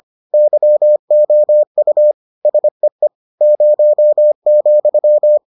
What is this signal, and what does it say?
Morse code, heard as a keyed tone, about 25 wpm, 600 Hz. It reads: YOU SEE 0,